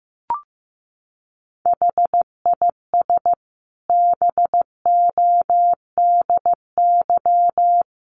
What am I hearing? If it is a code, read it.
Morse code, heard as a keyed tone, about 15 wpm, 700 Hz. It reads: HIS BODY
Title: A Sound Of Thunder 15wpm